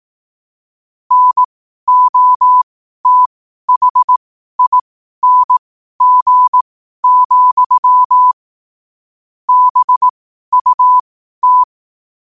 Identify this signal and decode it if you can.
Morse code, heard as a keyed tone, about 18 wpm, spaced as for 12 wpm, 1000 Hz. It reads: NOTHING, BUT
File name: qotdfiles / 12wpm